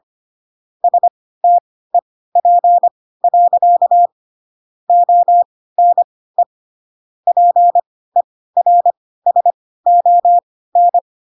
Morse code, heard as a keyed tone, about 25 wpm, spaced as for 16 wpm, 700 Hz. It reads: STEP. ONE PERSON
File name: Pinocchio / Pinocchio ch3 10 25 ik1hge